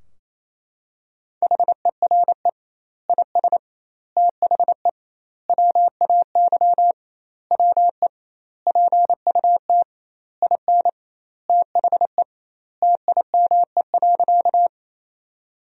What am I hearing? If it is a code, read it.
Morse code, heard as a keyed tone, about 27 wpm, 700 Hz. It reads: HERE IS THE WAY WE PUT IN THE TIME.
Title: 2x Word-Spacing - Sentences from Top 200 Words - 28wpm